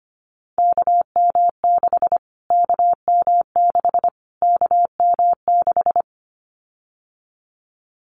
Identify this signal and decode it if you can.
Morse code, heard as a keyed tone, about 25 wpm, 700 Hz. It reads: KM6 KM6 KM6
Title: Callsign Prefixes for Top 25 DXCC - Repeated 3x - 25wpm